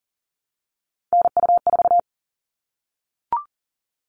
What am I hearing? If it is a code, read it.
Morse code, heard as a keyed tone, about 40 wpm, 700 Hz. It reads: NU4
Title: All Cut Numbers - 3-Digit 40wpm